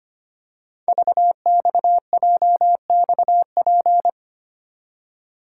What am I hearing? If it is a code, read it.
Morse code, heard as a keyed tone, about 25 wpm, 700 Hz. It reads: VXJXP
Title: Code Groups - 5 Letter:Numbers - 25wpm